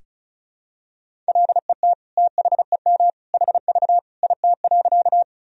Morse code, heard as a keyed tone, about 36 wpm, 700 Hz. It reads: LET THEM HV IT.
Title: Sentences from Top 300 Words and Abbreviations 35wpm